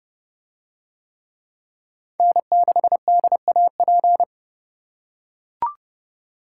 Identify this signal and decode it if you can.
Morse code, heard as a keyed tone, about 29 wpm, 700 Hz. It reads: N6DAP